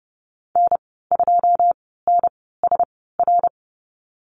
Morse code, heard as a keyed tone, about 30 wpm, spaced as for 17 wpm, 700 Hz. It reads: N2NSR
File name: Lesson 14g - Number 2 - Call Signs 30wpm